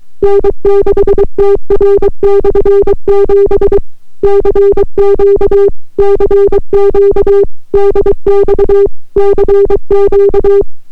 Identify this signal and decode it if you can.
Morse code, heard as a keyed tone, about 23 wpm, 400 Hz. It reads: N6TR/7 CQ CQ DX CQ